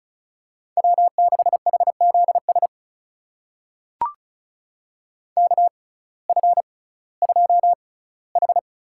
Morse code, heard as a keyed tone, about 36 wpm, spaced as for 19 wpm, 700 Hz. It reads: W6HZS KF2H
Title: USA Call Signs - 6x Character Spacing - 35wpm